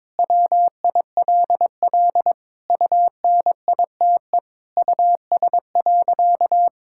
Morse code, heard as a keyed tone, about 22 wpm, 700 Hz. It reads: WILL UNITE US.
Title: Sentences from Top 1000 Words - Review Part 5 - 22wpm